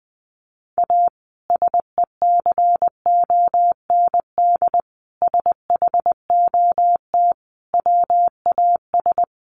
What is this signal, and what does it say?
Morse code, heard as a keyed tone, about 20 wpm, 700 Hz. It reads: A SECOND SHOT WAS
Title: Sentences from Top 1100 Words 20wpm